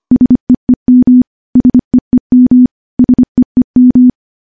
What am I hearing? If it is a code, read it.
Morse code, heard as a keyed tone, about 25 wpm, 250 Hz. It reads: SEEM SEEM SEEM
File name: morse / morse-training-25WPM